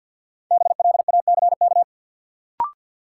Morse code, heard as a keyed tone, about 47 wpm, 700 Hz. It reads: BLACK
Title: Sets of 5 Words Encore - New Word Drill - 50wpm